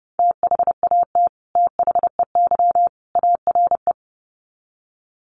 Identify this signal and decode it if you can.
Morse code, heard as a keyed tone, about 29 wpm, 700 Hz. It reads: THAT THEY ARE
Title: Sets of 3 Words 30wpm